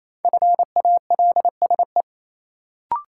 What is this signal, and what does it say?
Morse code, heard as a keyed tone, about 28 wpm, 700 Hz. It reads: FALSE
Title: Sentences from Top 1600 Words - New Word Drill - 28wpm